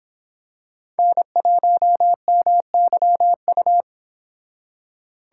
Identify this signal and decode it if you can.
Morse code, heard as a keyed tone, about 26 wpm, 700 Hz. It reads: N1MYU